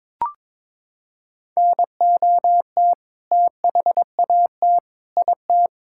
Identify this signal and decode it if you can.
Morse code, heard as a keyed tone, about 21 wpm, 700 Hz. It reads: NOT THAT IT
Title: Sentences from Top 1000 Words - Review Part 5 - 22wpm